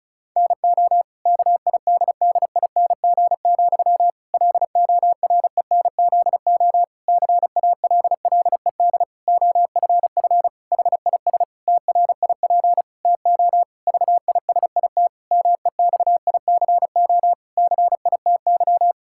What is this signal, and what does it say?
Morse code, heard as a keyed tone, about 36 wpm, 700 Hz. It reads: NO KIDDING, LORENZO CALLED OFF HIS TRIP TO VISIT MEXICO CITY
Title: Pangram Challenge v2 - 35wpm